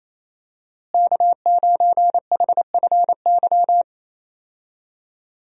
Morse code, heard as a keyed tone, about 28 wpm, 700 Hz. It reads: K9HFY
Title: USA Call Signs - 5 Characters - Repeated 2x - 28wpm